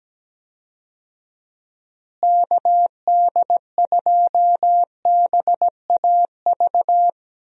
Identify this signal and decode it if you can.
Morse code, heard as a keyed tone, about 17 wpm, 700 Hz. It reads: KD2BAV
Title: USA Call Signs - 6 Characters - Repeated 3x - 17wpm